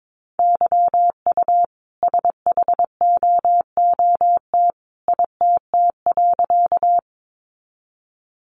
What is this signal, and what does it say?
Morse code, heard as a keyed tone, about 22 wpm, 700 Hz. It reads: YU SHOOT ITT.
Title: A Sound Of Thunder 22wpm